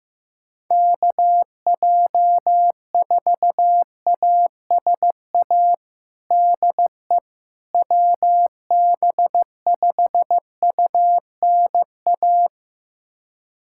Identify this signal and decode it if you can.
Morse code, heard as a keyed tone, about 15 wpm, 700 Hz. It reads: KJ4ASA DE WB5UNA
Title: USA Callsign DE Callsign - 6 Characters - 15wpm